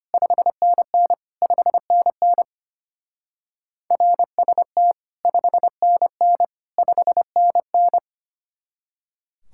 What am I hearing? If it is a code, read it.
Morse code, heard as a keyed tone, about 30 wpm, 700 Hz. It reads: 5NN 5NN RST 5NN 5NN
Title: Speed-Racing Common QSO Phrases - 25wpm